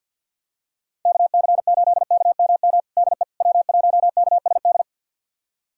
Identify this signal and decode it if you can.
Morse code, heard as a keyed tone, about 52 wpm, 700 Hz. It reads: KC8KMM DE W1KSD
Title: USA Callsign DE Callsign 50wpm